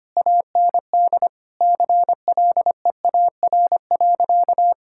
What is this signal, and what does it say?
Morse code, heard as a keyed tone, about 25 wpm, 700 Hz. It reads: AND CLEAR.
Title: Sentences from Top 1000 Words - Review Part 6 - 25wpm